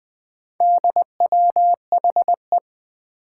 Morse code, heard as a keyed tone, about 20 wpm, 700 Hz. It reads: DWHE
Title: Four Challenging Letter-Number Combinations 20wpm